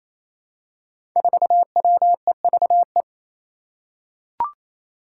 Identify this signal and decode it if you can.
Morse code, heard as a keyed tone, about 28 wpm, 700 Hz. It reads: 4WEVE
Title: Five Challenging Letter-Number Combinations 28wpm